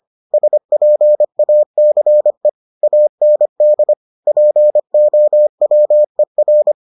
Morse code, heard as a keyed tone, about 25 wpm, 600 Hz. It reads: SPACE AND POWER